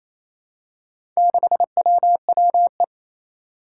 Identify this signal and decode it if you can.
Morse code, heard as a keyed tone, about 27 wpm, 700 Hz. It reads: 6WWE
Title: Four Challenging Letter-Number Combinations 28wpm